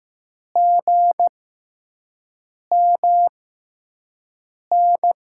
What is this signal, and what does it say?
Morse code, heard as a keyed tone, about 15 wpm, spaced as for 5 wpm, 700 Hz. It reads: GMN
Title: USA Call Signs - 5 Characters - 6x Character Spacing - 15wpm